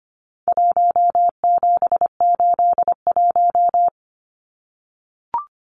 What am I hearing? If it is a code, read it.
Morse code, heard as a keyed tone, about 25 wpm, 700 Hz. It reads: 1781
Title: Numbers - Contest Style - 4-Digit 25wpm